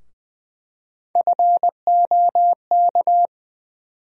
Morse code, heard as a keyed tone, about 20 wpm, 700 Hz. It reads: FOK
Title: USA Call Sign Postfixes - 3 Letters 20wpm